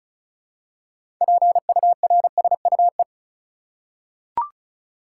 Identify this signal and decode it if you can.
Morse code, heard as a keyed tone, about 36 wpm, 700 Hz. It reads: PURSUE
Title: Six Letter Words - 35wpm